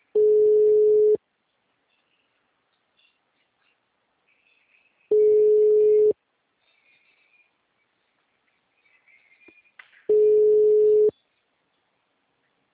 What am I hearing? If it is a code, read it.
Ringback tone (Europe, 425 Hz).
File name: out- -2009- - - .24